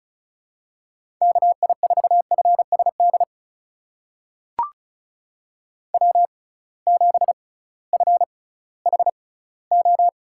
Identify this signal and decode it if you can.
Morse code, heard as a keyed tone, about 36 wpm, spaced as for 19 wpm, 700 Hz. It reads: KI4FSD W7FHO
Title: USA Call Signs - 6x Character Spacing - 35wpm